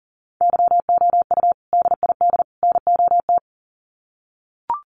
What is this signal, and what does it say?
Morse code, heard as a keyed tone, about 37 wpm, 700 Hz. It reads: YOU DID NOT
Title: Sets of 3 Words 40wpm